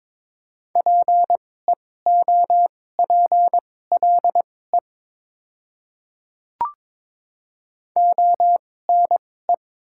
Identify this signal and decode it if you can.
Morse code, heard as a keyed tone, about 22 wpm, spaced as for 16 wpm, 690 Hz. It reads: PEOPLE ONE
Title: 2x Character Spacing - Top 100 Words - 22wpm